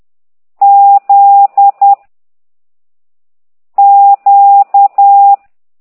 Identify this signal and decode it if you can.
Morse code, heard as a keyed tone, about 10 wpm, 800 Hz. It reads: Z Q